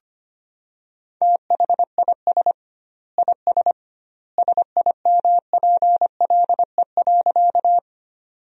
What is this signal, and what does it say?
Morse code, heard as a keyed tone, about 25 wpm, 700 Hz. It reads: THIS IS SIMPLE.
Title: 2x Word-Spacing - Sentences from Top 400 Words - 25wpm